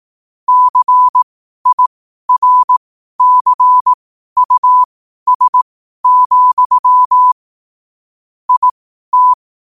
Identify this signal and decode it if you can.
Morse code, heard as a keyed tone, about 18 wpm, spaced as for 12 wpm, 1000 Hz. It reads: CIRCUS, IT